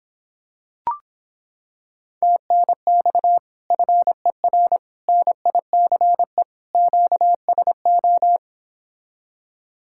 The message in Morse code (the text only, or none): TNX FER NICE QSO